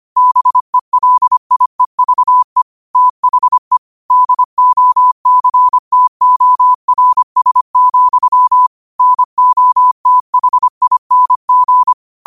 Morse code, heard as a keyed tone, about 25 wpm, 1000 Hz. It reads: BELIEVE THE DOCTORS, NOTHING
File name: qotdfiles / 25wpm